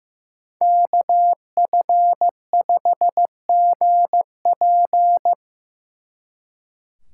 Morse code, heard as a keyed tone, about 18 wpm, 700 Hz. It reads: KF5GP
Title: Speed Racing - USA Call Signs 15wpm